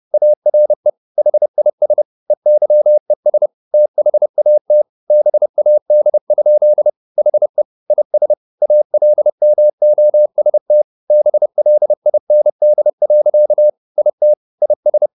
Morse code, heard as a keyed tone, about 30 wpm, 600 Hz. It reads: ARE HIS EYES THAT BAD? HE IS ALMOST BLIND. IT IS
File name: the old man and the sea / the old man and the sea 1 30wpm